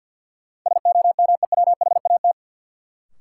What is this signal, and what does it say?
Morse code, heard as a keyed tone, about 50 wpm, 700 Hz. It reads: SOMEWHAT
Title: Sentences from Top 1000 Words - Review Part 1 - New Word Drill - 50wpm